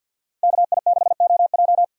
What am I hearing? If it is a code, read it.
Morse code, heard as a keyed tone, about 47 wpm, 700 Hz. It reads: KI6OJ
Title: USA Call Signs - 5 Characters - 50wpm